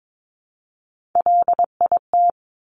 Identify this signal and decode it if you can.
Morse code, heard as a keyed tone, about 22 wpm, 700 Hz. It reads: LIT
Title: Top Three Letter Combinations in Words 22wpm